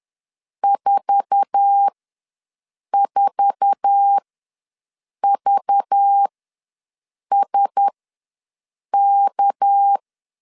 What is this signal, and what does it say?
Morse code, heard as a keyed tone, about 10 wpm, spaced as for 6 wpm, 800 Hz. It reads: 44VSK
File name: LEZIONI / 8 YJ45 / 8 YJ4 ws 10wpm 4020